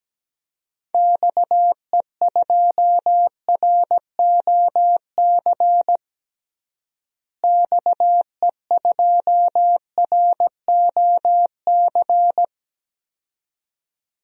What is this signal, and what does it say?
Morse code, heard as a keyed tone, about 17 wpm, 700 Hz. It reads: XE2ROC XE2ROC
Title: DXCC-100 Call Signs - 6 Characters - Repeated 2x - 17wpm